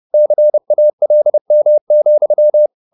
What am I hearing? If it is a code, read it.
Morse code, heard as a keyed tone, about 30 wpm, 600 Hz. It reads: CALM,